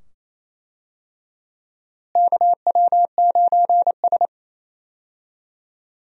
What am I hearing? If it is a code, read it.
Morse code, heard as a keyed tone, about 28 wpm, 700 Hz. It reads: KW9S
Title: USA Call Signs - 4 Characters - Repeated 2x - 28wpm